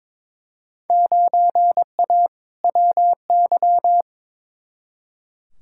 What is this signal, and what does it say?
Morse code, heard as a keyed tone, about 22 wpm, 700 Hz. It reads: 9A WY
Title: Field Day Exchanges Encore - 22wpm